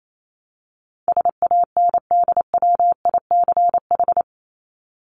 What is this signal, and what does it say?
Morse code, heard as a keyed tone, about 28 wpm, 700 Hz. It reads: SANDWICH
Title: Top 1751-2000 Words 28wpm